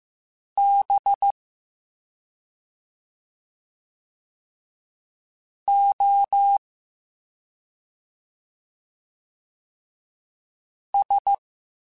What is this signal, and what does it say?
Morse code, heard as a keyed tone, about 15 wpm, 800 Hz. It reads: B O S